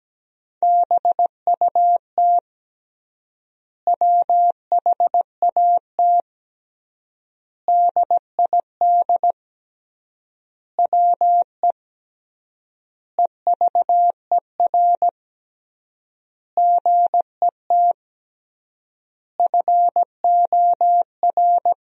Morse code, heard as a keyed tone, about 17 wpm, 700 Hz. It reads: BUT WHAT DID WE EVER GET FOR
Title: 3x Word-Spacing - Sentences from Top 400 Words - 17wpm